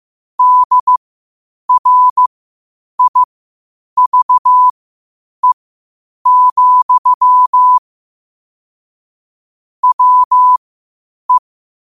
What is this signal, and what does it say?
Morse code, heard as a keyed tone, about 15 wpm, spaced as for 8 wpm, 1000 Hz. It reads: DRIVE, WE
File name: qotdfiles / 8wpm